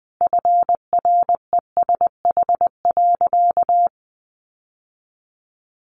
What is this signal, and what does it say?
Morse code, heard as a keyed tone, about 20 wpm, 700 Hz. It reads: FRESH.
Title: Sentences from Top 1200 Words 20wpm